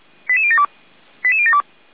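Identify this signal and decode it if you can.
Five-tone selective call (ZVEI), heard as quick runs of five tones, tones of about 70 ms. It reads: 88072 88072
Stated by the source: ZVEI five-tone alert received by SDR, fire brigade feuerwehr-orlamuende.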